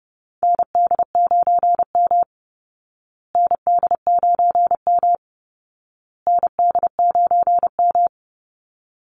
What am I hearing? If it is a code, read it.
Morse code, heard as a keyed tone, about 30 wpm, 700 Hz. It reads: ND9M ND9M ND9M
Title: Call Signs Repeated 2x - 30wpm